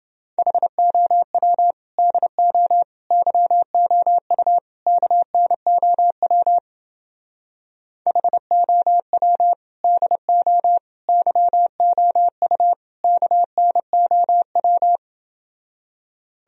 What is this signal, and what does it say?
Morse code, heard as a keyed tone, about 29 wpm, 700 Hz. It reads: HOW DO YOU KNOW HOW DO YOU KNOW